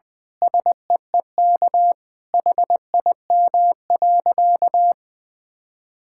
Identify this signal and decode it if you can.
Morse code, heard as a keyed tone, about 21 wpm, 700 Hz. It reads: SEEK HIM.
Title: Sentences from Top 1600 Words 20wpm